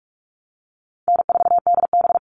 Morse code, heard as a keyed tone, about 44 wpm, 700 Hz. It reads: N4DB
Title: All Cut Numbers - 4-Digit 45wpm